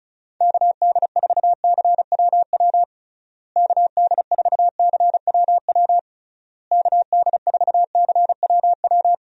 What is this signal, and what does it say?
Morse code, heard as a keyed tone, about 34 wpm, 700 Hz. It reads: KD4CWW KD4CWW KD4CWW
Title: USA Call Signs - 6 Characters - Repeated 3x - 35wpm